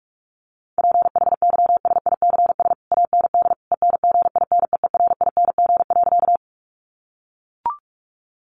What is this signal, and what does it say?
Morse code, heard as a keyed tone, about 44 wpm, 700 Hz. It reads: PHYSICS AND ENGINEERING.
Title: Sentences from Top 1500 Words 45wpm